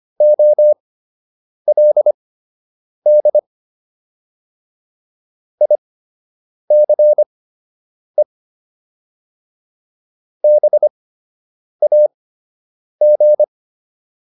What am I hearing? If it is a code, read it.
Morse code, heard as a keyed tone, about 25 wpm, spaced as for 8 wpm, 600 Hz. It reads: OLD ICE BAG